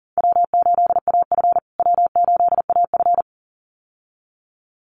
Morse code, heard as a keyed tone, about 40 wpm, 700 Hz. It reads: W8AF W8AF
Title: USA Call Signs Repeated 2x 40wpm